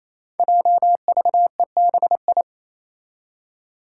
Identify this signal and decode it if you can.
Morse code, heard as a keyed tone, about 28 wpm, 700 Hz. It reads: JVEBI
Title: Five Challenging Letter-Number Combinations 28wpm